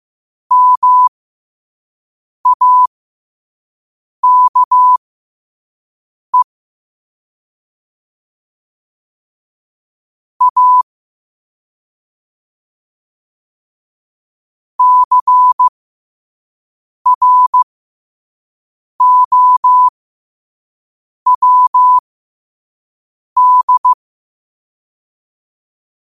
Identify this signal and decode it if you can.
Morse code, heard as a keyed tone, about 15 wpm, spaced as for 5 wpm, 1000 Hz. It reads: MAKE A CROWD
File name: qotdfiles / 5wpm